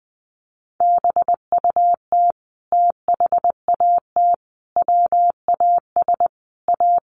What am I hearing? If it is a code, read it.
Morse code, heard as a keyed tone, about 20 wpm, 700 Hz. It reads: BUT THAT WAS A
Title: Sentences from Top 1100 Words 20wpm